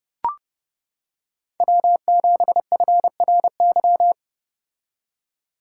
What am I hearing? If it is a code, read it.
Morse code, heard as a keyed tone, about 29 wpm, 700 Hz. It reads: W7FRY